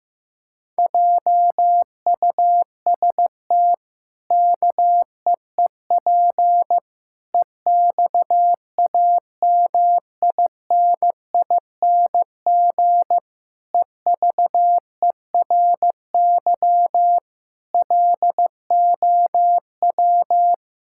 Morse code, heard as a keyed tone, about 15 wpm, 700 Hz. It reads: JUST KEEP EXAMINING EVERY LOW
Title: Pangram Challenge v2 - 15wpm